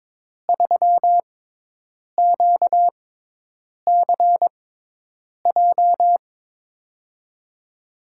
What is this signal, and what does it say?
Morse code, heard as a keyed tone, about 22 wpm, spaced as for 8 wpm, 700 Hz. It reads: 3QCJ
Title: USA Call Signs - 5 Characters - 6x Character Spacing - 22wpm